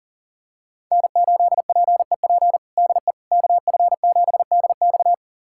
Morse code, heard as a keyed tone, about 40 wpm, 700 Hz. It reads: N8PEP DE KF7DX
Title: USA Callsign DE Callsign - 5 Characters - 40wpm